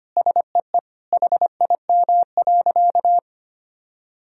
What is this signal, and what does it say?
Morse code, heard as a keyed tone, about 25 wpm, 700 Hz. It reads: SEE HIM.
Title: Sentences from Top 1000 Words - Review Part 2 - 25wpm